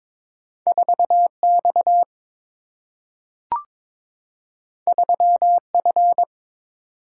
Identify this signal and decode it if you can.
Morse code, heard as a keyed tone, about 23 wpm, 700 Hz. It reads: 4X 3F
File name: practice / 22wpm / Two Letter-Number Combinations - 22wpm